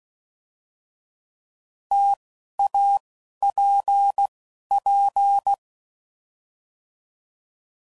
Morse code, heard as a keyed tone, about 16 wpm, 800 Hz. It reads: T A P P